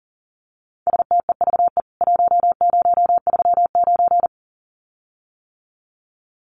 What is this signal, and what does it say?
Morse code, heard as a keyed tone, about 40 wpm, 700 Hz. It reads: STEVE 1039
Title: CWT Exchange 40wpm